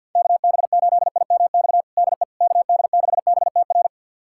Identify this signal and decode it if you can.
Morse code, heard as a keyed tone, about 50 wpm, 690 Hz. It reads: KB8IMX DE KD6BTR
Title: USA Callsign DE Callsign 50wpm